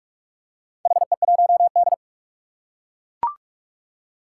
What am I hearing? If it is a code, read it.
Morse code, heard as a keyed tone, about 41 wpm, 700 Hz. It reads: HE1D
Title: Four Challenging Letter-Number Combinations 45wpm